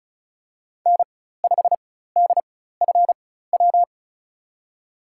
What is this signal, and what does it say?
Morse code, heard as a keyed tone, about 34 wpm, spaced as for 16 wpm, 700 Hz. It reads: N5DFW
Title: USA Call Signs - 4x Character Spacing - 35wpm